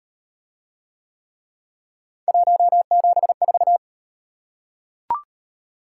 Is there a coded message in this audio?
Yes: Morse code from 2.3 to 3.8 s, heard as a keyed tone, about 37 wpm, 700 Hz. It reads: 174